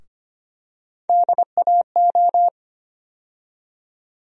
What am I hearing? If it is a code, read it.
Morse code, heard as a keyed tone, about 25 wpm, 700 Hz. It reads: DAO
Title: USA Call Sign Postfixes - 3 Letters 25wpm